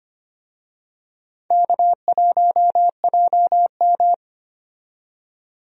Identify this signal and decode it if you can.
Morse code, heard as a keyed tone, about 25 wpm, 700 Hz. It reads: K1JM
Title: USA Call Signs - 4 Characters - 25wpm